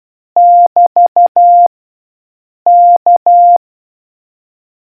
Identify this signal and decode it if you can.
Morse code, heard as a keyed tone, about 12 wpm, 700 Hz. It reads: <BT> K